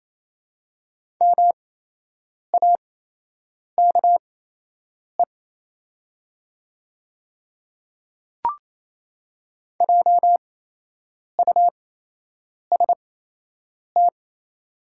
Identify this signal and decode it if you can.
Morse code, heard as a keyed tone, about 28 wpm, spaced as for 8 wpm, 700 Hz. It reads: MAKE JUST
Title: 8x Character Spacing - Top 200 Words - 28wpm